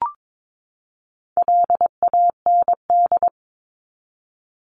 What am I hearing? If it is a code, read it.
Morse code, heard as a keyed tone, about 22 wpm, 700 Hz. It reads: LAND
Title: Top 500 Words 22wpm